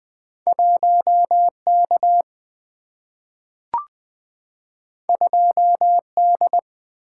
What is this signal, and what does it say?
Morse code, heard as a keyed tone, about 20 wpm, 700 Hz. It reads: 1K 2D
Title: Single-Letter Single-Number 20wpm